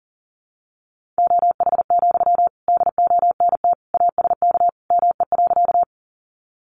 Morse code, heard as a keyed tone, about 40 wpm, 700 Hz. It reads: OH, DONT ASK ME.
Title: Sentences from Top 500 Words 40wpm